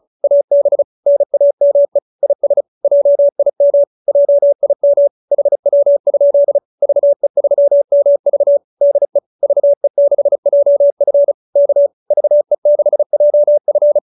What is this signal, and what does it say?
Morse code, heard as a keyed tone, about 34 wpm, 590 Hz. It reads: AB NAME IS JIM JIM HW? VE3MV DE VE6JF K VE6JF